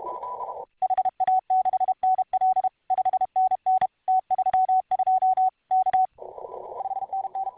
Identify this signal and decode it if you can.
Morse code, heard as a keyed tone, about 33 wpm, 750 Hz. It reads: HA6NL 5NN T32 K 5NN